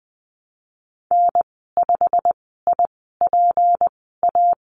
Morse code, heard as a keyed tone, about 20 wpm, spaced as for 15 wpm, 700 Hz. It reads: N5IPA
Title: 161 Lesson 26g - Letter g - Call Signs 20wpm